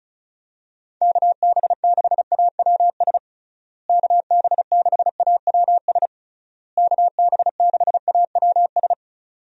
Morse code, heard as a keyed tone, about 34 wpm, 700 Hz. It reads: KB6AWS KB6AWS KB6AWS
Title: USA Call Signs - 6 Characters - Repeated 3x - 35wpm